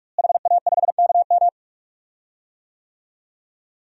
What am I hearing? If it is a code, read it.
Morse code, heard as a keyed tone, about 46 wpm, 700 Hz. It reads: HA5KM
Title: DXCC-100 Call Signs - 5 Characters - Repeated 2x - 45wpm